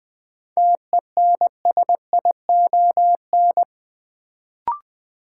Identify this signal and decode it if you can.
Morse code, heard as a keyed tone, about 20 wpm, 700 Hz. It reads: TENSION
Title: Sentences from Top 1600 Words - New Word Drill - 20wpm